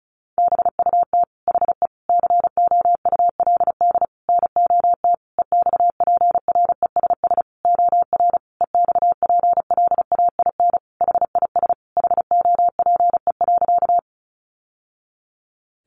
Morse code, heard as a keyed tone, about 34 wpm, 700 Hz. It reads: BUT HE COULD NOT EXPRESS OR EXPLAIN HIS HOPE.
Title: Sentences from Top 1100 Words 35wpm